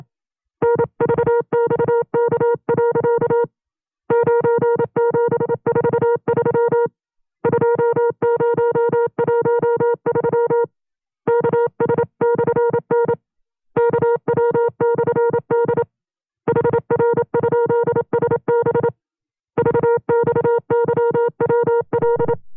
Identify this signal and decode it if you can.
Morse code, heard as a keyed tone, about 27 wpm, 470 Hz. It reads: NVXK. 9743 2013 KS/N KW/D HR?SB VXYWL
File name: 25WPM